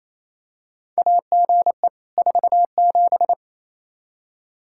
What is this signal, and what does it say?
Morse code, heard as a keyed tone, about 28 wpm, 700 Hz. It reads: AGE 47
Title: Practice QSO - Exchange 3 - More Info - 28wpm